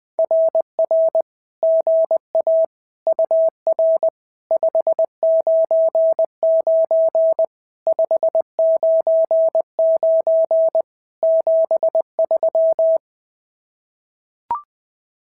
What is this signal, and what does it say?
Morse code, heard as a keyed tone, about 20 wpm, 650 Hz. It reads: RR GA UR 599 599 73
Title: SOTA QSO Practice 20wpm